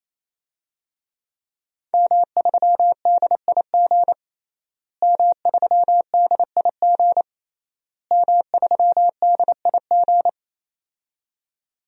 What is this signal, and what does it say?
Morse code, heard as a keyed tone, about 28 wpm, 700 Hz. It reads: M3DIG M3DIG M3DIG
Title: DXCC-100 Call Signs - 5 Characters - Repeated 3x - 28wpm